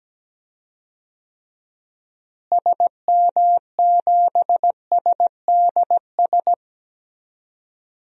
Morse code, heard as a keyed tone, about 17 wpm, 700 Hz. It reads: SM7SDS
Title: DXCC-100 Call Signs - 6 Characters - Repeated 3x - 17wpm